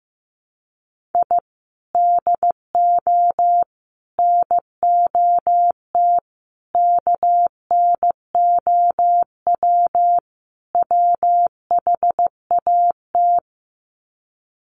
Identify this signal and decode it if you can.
Morse code, heard as a keyed tone, about 15 wpm, 700 Hz. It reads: I DO NOT KNOW WHAT
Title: Sets of 5 Words 15wpm